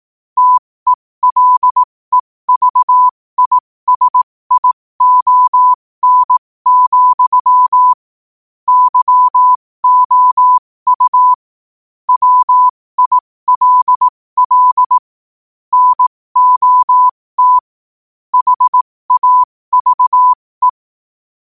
Morse code, heard as a keyed tone, about 18 wpm, spaced as for 15 wpm, 1000 Hz. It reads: TELEVISION, YOU WILL NOT HAVE